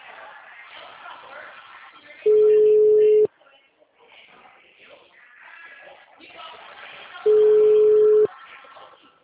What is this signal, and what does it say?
Ringback tone (Europe, 425 Hz), with speech.
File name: out- -2048- - - .46544